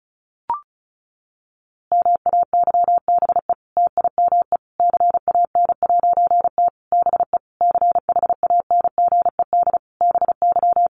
Morse code, heard as a keyed tone, about 34 wpm, 700 Hz. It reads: MAYBE TIME CAN'T BE CHANGED BY
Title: A Sound Of Thunder 35wpm